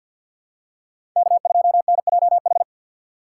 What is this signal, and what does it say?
Morse code, heard as a keyed tone, about 53 wpm, 700 Hz. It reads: K2NJH
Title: USA Call Signs - 5 Characters - 3x Character Spacing - 50wpm